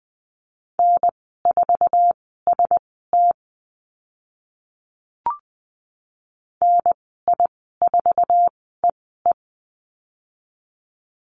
Morse code, heard as a keyed tone, about 20 wpm, spaced as for 14 wpm, 700 Hz. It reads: N4ST NI4EE
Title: 048 Lesson 9g - Number 4 - Call Signs 20wpm